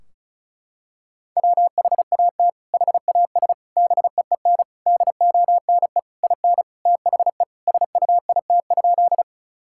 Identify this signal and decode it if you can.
Morse code, heard as a keyed tone, about 36 wpm, 700 Hz. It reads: WHAT HAS BEEN DONE IN THE SUIT?
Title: Sentences from Top 1600 Words 35wpm